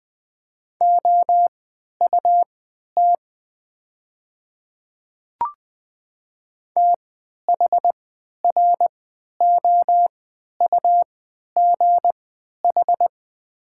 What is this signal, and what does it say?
Morse code, heard as a keyed tone, about 20 wpm, spaced as for 11 wpm, 700 Hz. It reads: OUT THROUGH
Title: 3x Character Spacing - Top 200 Words - 20wpm